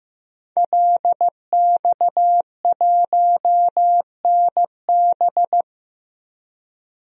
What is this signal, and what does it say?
Morse code, heard as a keyed tone, about 15 wpm, 700 Hz. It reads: LX1NB
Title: DXCC-100 Call Signs - 5 Characters - 15wpm